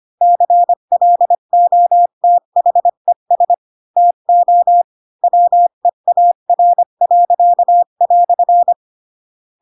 Morse code, heard as a keyed tone, about 25 wpm, 700 Hz. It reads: CLOTHES TO WEAR."
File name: Pinocchio / Pinocchio ch8 20 25 ik1hge